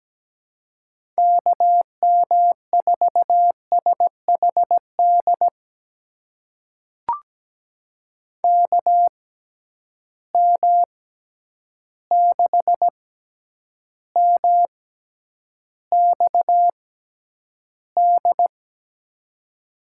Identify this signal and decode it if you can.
Morse code, heard as a keyed tone, about 17 wpm, spaced as for 9 wpm, 700 Hz. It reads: KM4SHD KM6MXD
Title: USA Call Signs - 6 Characters - 6x Character Spacing - 17wpm